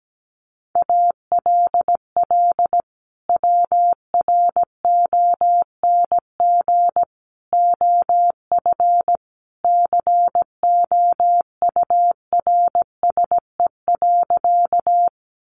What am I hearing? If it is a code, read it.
Morse code, heard as a keyed tone, about 17 wpm, 700 Hz. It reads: ALL WRONG OF COURSE.
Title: Sentences from Top 900 Words 17wpm